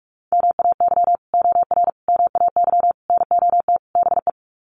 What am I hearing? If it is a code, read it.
Morse code, heard as a keyed tone, about 44 wpm, 700 Hz. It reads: MAY OR MAY NOT BE
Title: Sets of 5 Words 45wpm